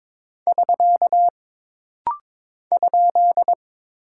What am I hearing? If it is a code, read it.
Morse code, heard as a keyed tone, about 22 wpm, 700 Hz. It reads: <SK> ?
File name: practice / 22wpm / Single Punctuation-Prosign - Rapid-Fire - 22wpm